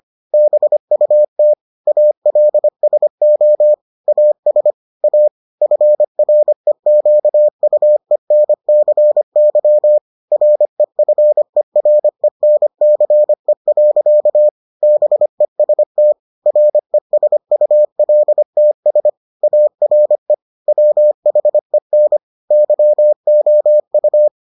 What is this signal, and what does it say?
Morse code, heard as a keyed tone, about 25 wpm, 600 Hz. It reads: BUT ALSO AS A FREQUENCY REFERENCE. BEST RESULTS ARE WHEN YOU